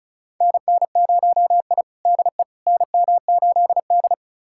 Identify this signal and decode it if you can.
Morse code, heard as a keyed tone, about 36 wpm, 700 Hz. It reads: NN0I DE NM8D
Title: USA Callsign DE Callsign - 4 Characters - 35wpm